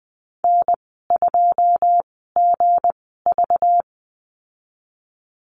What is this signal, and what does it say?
Morse code, heard as a keyed tone, about 19 wpm, spaced as for 14 wpm, 700 Hz. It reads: N2GV
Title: 213 Lesson 34g - Number 8 - Call Signs 20wpm